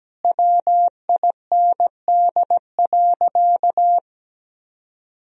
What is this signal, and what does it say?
Morse code, heard as a keyed tone, about 17 wpm, 700 Hz. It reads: WIND.
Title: Sentences from Top 1000 Words - Review Part 4 - 17wpm